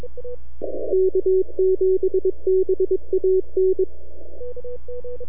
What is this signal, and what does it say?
Morse code, heard as a keyed tone, about 21 wpm, 380 Hz. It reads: K7BAN
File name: NAQPCW-8-2009 / W1UJ NAQP CW 09 1930Z